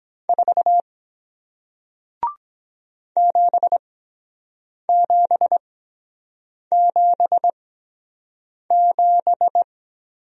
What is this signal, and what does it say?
Morse code, heard as a keyed tone, about 23 wpm, spaced as for 7 wpm, 700 Hz. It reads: 4 7777